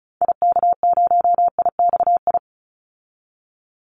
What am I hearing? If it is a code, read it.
Morse code, heard as a keyed tone, about 34 wpm, 700 Hz. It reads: IK0IXI
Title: Call Signs Repeated 3x - 35wpm